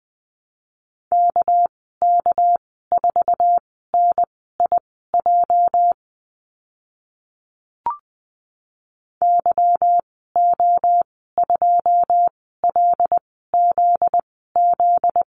Morse code, heard as a keyed tone, about 20 wpm, spaced as for 14 wpm, 700 Hz. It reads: KK4NIJ YO2LZZ
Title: 245 Lesson 39g - Letter z - Call Signs 20wpm